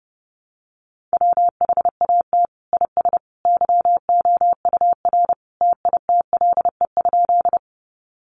Morse code, heard as a keyed tone, about 30 wpm, 700 Hz. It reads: WHAT IS YOUR TITLE?
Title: Sentences from Top 1200 Words 30wpm